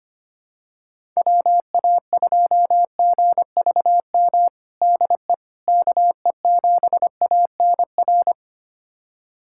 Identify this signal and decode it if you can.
Morse code, heard as a keyed tone, about 25 wpm, 700 Hz. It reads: WA2GVM DE KE7ANR
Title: USA Callsign DE Callsign - 6 Characters - 25wpm